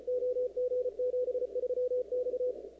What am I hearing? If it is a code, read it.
Morse code, heard as a keyed tone, about 33 wpm, 500 Hz. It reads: OG73X
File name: contests / 2019-cw-wpx-kt5j-k5pi- -2330 0-38